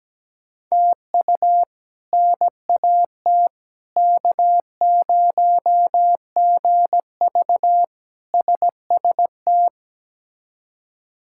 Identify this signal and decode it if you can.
Morse code, heard as a keyed tone, about 17 wpm, 700 Hz. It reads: TU NAT K0GV SST
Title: K1USN SST QSO Practice - 17wpm